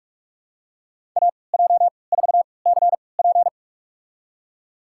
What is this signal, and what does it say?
Morse code, heard as a keyed tone, about 44 wpm, spaced as for 25 wpm, 700 Hz. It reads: AJ4CP
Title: USA Call Signs - 3x Character Spacing - 45wpm